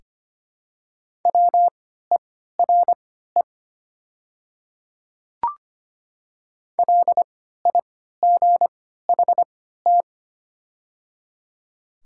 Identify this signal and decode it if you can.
Morse code, heard as a keyed tone, about 25 wpm, spaced as for 14 wpm, 700 Hz. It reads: WERE LIGHT
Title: 3x Character Spacing - Top 200 Words - 25wpm